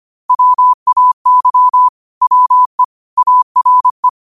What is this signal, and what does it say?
Morse code, heard as a keyed tone, about 25 wpm, 1000 Hz. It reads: WAY WE ARE
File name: 25wpm